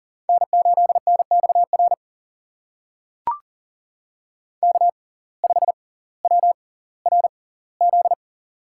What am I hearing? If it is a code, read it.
Morse code, heard as a keyed tone, about 40 wpm, spaced as for 20 wpm, 700 Hz. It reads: N8NXR K5WRZ
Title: USA Call Signs - 5 Characters - 6x Character Spacing - 40wpm